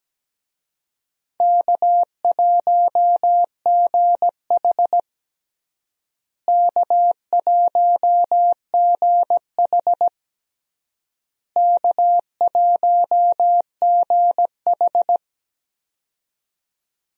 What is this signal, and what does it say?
Morse code, heard as a keyed tone, about 17 wpm, 700 Hz. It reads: K1GH K1GH K1GH
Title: USA Call Signs - 4 Characters - Repeated 3x - 17wpm